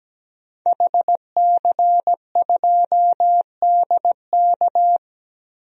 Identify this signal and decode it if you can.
Morse code, heard as a keyed tone, about 17 wpm, 700 Hz. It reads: HC2DK
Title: DXCC-100 Call Signs - 5 Characters - Repeated 3x - 17wpm